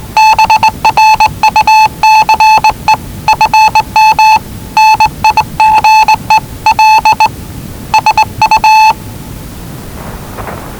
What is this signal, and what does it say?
Morse code, heard as a keyed tone, about 20 wpm, 870 Hz. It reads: BRUCE FM NIGEL SU
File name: WellingtonRadioZlwCloseDownOnHfAndMf / 01-Zlw 4277khz 417.5 qrt until 0925z